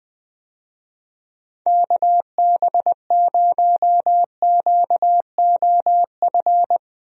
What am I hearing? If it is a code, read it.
Morse code, heard as a keyed tone, about 21 wpm, 700 Hz. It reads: KB0QOF
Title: USA Call Signs - 6 Characters - 20wpm